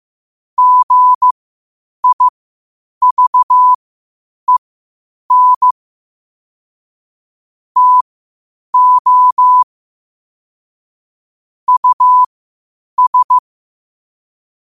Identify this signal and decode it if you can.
Morse code, heard as a keyed tone, about 15 wpm, spaced as for 8 wpm, 1000 Hz. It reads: GIVEN TO US
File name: qotdfiles / 8wpm